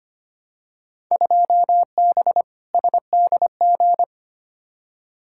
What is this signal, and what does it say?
Morse code, heard as a keyed tone, about 25 wpm, 700 Hz. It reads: 2B SDG
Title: Field Day Exchanges Encore - 25wpm